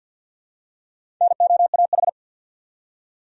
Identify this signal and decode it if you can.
Morse code, heard as a keyed tone, about 51 wpm, 700 Hz. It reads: NOAH
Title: USA Common Names - Max 4 Letters - 50wpm